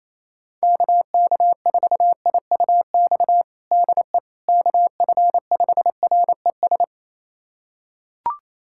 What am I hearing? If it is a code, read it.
Morse code, heard as a keyed tone, about 28 wpm, 700 Hz. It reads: KK4IUX DE KF5RES
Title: USA Callsign DE Callsign 28wpm